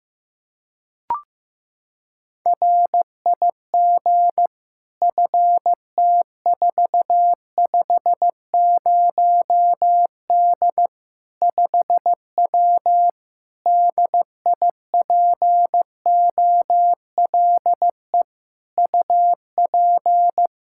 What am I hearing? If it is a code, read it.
Morse code, heard as a keyed tone, about 15 wpm, 700 Hz. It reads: RIG FT450D 5W DIPOLE UP
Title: Practice QSO - Exchange 2 - Info - 15wpm